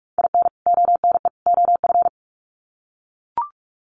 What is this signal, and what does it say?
Morse code, heard as a keyed tone, about 44 wpm, 700 Hz. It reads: IN ONE OF
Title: Sets of 3 Words 45wpm